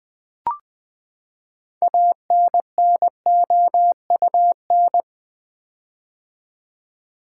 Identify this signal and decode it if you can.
Morse code, heard as a keyed tone, about 20 wpm, 700 Hz. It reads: ANNOUN
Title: Build A Word - 6 Letters At A Time - 12-Letter Words - 20wpm